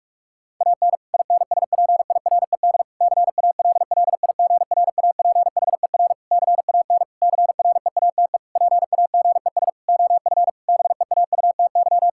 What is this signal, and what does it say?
Morse code, heard as a keyed tone, about 46 wpm, 700 Hz. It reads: AN INSPIRED CALLIGRAPHER CAN CREATE PAGES OF BEAUTY
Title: Pangram Challenge v2 - 45wpm